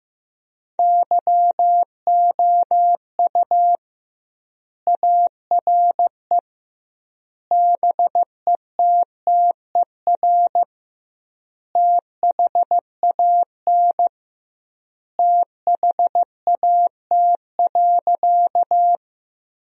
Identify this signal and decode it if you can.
Morse code, heard as a keyed tone, about 15 wpm, 700 Hz. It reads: YOU ARE BETTER THAN THAT.
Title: 2x Word-Spacing - Sentences from Top 400 Words - 15wpm